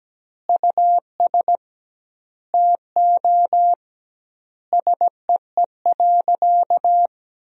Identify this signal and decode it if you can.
Morse code, heard as a keyed tone, about 17 wpm, 700 Hz. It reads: US TO SEE.
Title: 2x Word-Spacing - Sentences from Top 500 Words - 17wpm